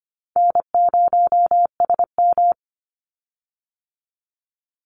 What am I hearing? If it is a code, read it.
Morse code, heard as a keyed tone, about 25 wpm, 700 Hz. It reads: N0SM
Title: Easy Call Signs - 25wpm